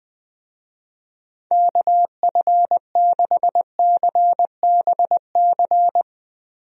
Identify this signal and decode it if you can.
Morse code, heard as a keyed tone, about 20 wpm, 700 Hz. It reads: KF6CBC
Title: USA Call Signs - 6 Characters - Repeated 2x - 20wpm